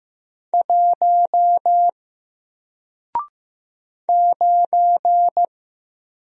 Morse code, heard as a keyed tone, about 15 wpm, 700 Hz. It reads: 1 9
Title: Single Letter-Number 15wpm